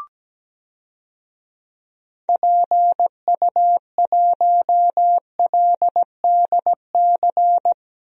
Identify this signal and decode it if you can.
Morse code, heard as a keyed tone, about 17 wpm, 700 Hz. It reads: PU1LDC
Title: DXCC-100 Call Signs - 6 Characters - Repeated 2x - 17wpm